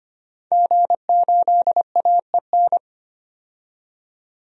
Morse code, heard as a keyed tone, about 25 wpm, 700 Hz. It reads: G8AEN
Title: DXCC-100 Call Signs - 5 Characters - 25wpm